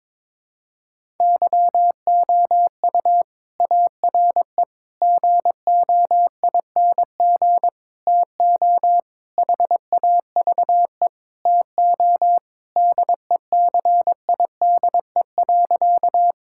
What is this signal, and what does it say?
Morse code, heard as a keyed tone, about 22 wpm, 700 Hz. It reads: YOU ARE GOING TO HAVE TO DECIDE.
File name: practice / 22wpm / Sentences with 4-Word Phrases - 22wpm